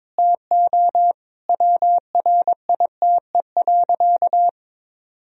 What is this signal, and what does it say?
Morse code, heard as a keyed tone, about 21 wpm, 700 Hz. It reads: TO WRITE.
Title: Sentences from Top 1000 Words - Review Part 1 - 22wpm